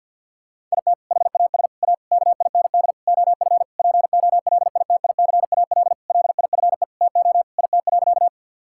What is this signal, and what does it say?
Morse code, heard as a keyed tone, about 50 wpm, 700 Hz. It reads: IT HAS A KIND OF POLITICAL LIFE TO IT.
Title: Sentences from Top 1000 Words - Review Part 5 - 50wpm